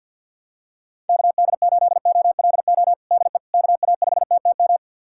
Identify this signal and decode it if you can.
Morse code, heard as a keyed tone, about 51 wpm, 700 Hz. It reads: KD8OLO DE KA5TTM
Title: USA Callsign DE Callsign 50wpm